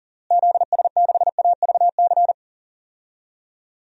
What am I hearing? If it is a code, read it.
Morse code, heard as a keyed tone, about 40 wpm, 700 Hz. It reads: ZS6AVC
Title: DXCC-100 Call Signs - 6 Characters - Repeated 3x - 40wpm